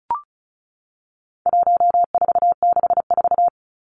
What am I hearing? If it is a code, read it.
Morse code, heard as a keyed tone, about 34 wpm, 700 Hz. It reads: 1464
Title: Numbers - 4-Digit 35wpm